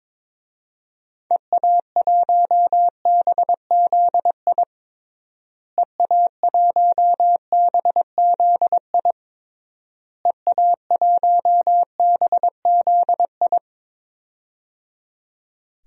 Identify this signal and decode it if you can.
Morse code, heard as a keyed tone, about 22 wpm, 700 Hz. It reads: EA1BZI EA1BZI EA1BZI
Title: DXCC-100 Call Signs - 6 Characters - Repeated 3x - 22wpm